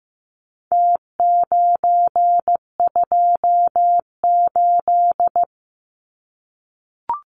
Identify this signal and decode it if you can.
Morse code, heard as a keyed tone, about 15 wpm, 700 Hz. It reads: T928
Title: Numbers - Contest Style - 4-Digit 15wpm